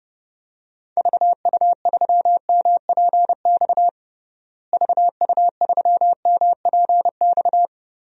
Morse code, heard as a keyed tone, about 32 wpm, 700 Hz. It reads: VU3MPX VU3MPX
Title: DXCC-100 Call Signs - 6 Characters - Repeated 2x - 30wpm